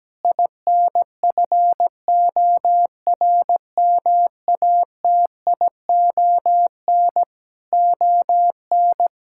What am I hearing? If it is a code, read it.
Morse code, heard as a keyed tone, about 17 wpm, 700 Hz. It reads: INFORMATION ON
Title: Sentences from Top 1000 Words - Review Part 6 - 17wpm